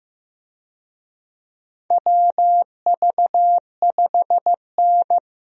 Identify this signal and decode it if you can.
Morse code, heard as a keyed tone, about 15 wpm, 700 Hz. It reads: WV5N
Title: USA Call Signs - 4 Characters - Repeated 3x - 15wpm